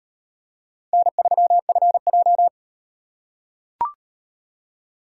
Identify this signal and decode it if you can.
Morse code, heard as a keyed tone, about 38 wpm, 700 Hz. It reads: N3FJ